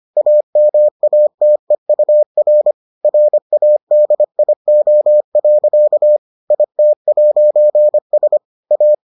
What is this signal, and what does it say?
Morse code, heard as a keyed tone, about 25 wpm, 600 Hz. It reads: AMATEUR RADIO. IT'S A